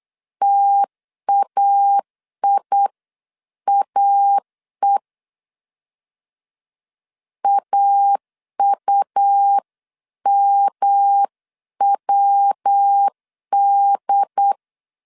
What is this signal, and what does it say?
Morse code, heard as a keyed tone, about 8 wpm, 800 Hz. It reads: TAIAE AUMWD